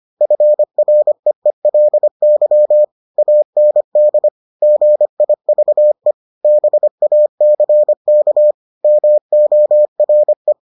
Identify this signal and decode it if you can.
Morse code, heard as a keyed tone, about 25 wpm, 600 Hz. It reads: FREELY AND GIVE BACK MORE